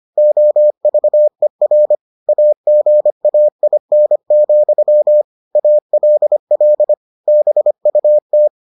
Morse code, heard as a keyed tone, about 25 wpm, 600 Hz. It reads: OVER AGAIN, ALL BUT